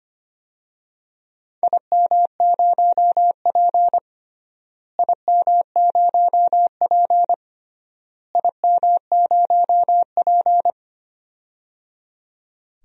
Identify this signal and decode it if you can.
Morse code, heard as a keyed tone, about 25 wpm, 700 Hz. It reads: IM0P IM0P IM0P
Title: DXCC-100 Call Signs - 4 Characters - Repeated 3x - 25wpm